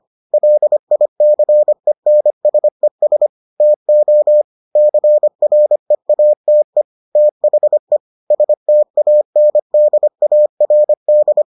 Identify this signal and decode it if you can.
Morse code, heard as a keyed tone, about 25 wpm, 600 Hz. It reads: LICENSES TO CREATE THE STANDARD